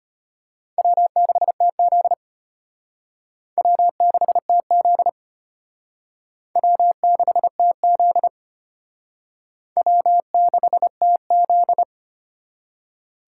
Morse code, heard as a keyed tone, about 32 wpm, 700 Hz. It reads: W6TZ W6TZ W6TZ W6TZ